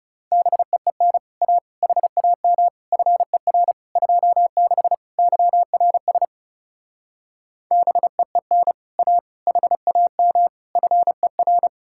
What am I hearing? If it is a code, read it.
Morse code, heard as a keyed tone, about 38 wpm, 700 Hz. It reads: BEEN A HAM FER 26 YRS BEEN A HAM FER